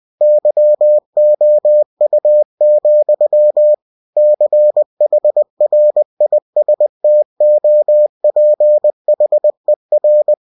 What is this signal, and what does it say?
Morse code, heard as a keyed tone, about 20 wpm, 600 Hz. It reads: YOU, CHRISTOPHER
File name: cw-books / winnie-the-pooh / winnie-20wpm-page-0088